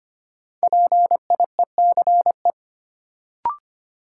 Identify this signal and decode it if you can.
Morse code, heard as a keyed tone, about 25 wpm, 700 Hz. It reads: PIECE
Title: Sets of 3 Words - New Word Drill - 25wpm